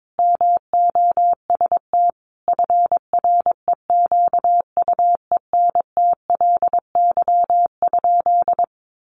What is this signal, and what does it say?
Morse code, heard as a keyed tone, about 22 wpm, 700 Hz. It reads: MOST FREQUENTLY?
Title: Sentences from Top 1300 Words 22wpm